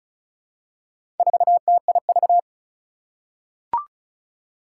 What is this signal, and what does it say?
Morse code, heard as a keyed tone, about 36 wpm, 690 Hz. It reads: 4TIV
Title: Four Challenging Letter-Number Combinations 35wpm